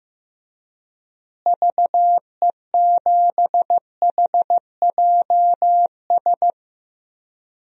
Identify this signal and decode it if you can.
Morse code, heard as a keyed tone, about 15 wpm, 700 Hz. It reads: VE7HJS
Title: DXCC-100 Call Signs - 6 Characters - Repeated 2x - 15wpm